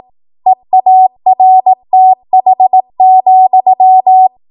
Morse code, heard as a keyed tone, about 18 wpm, 750 Hz. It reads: EARTH,
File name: Morse / 18WPM